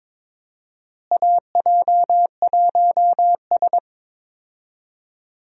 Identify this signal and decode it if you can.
Morse code, heard as a keyed tone, about 22 wpm, 700 Hz. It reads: AJ1S
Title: USA Call Signs - 4 Characters - Repeated 2x - 22wpm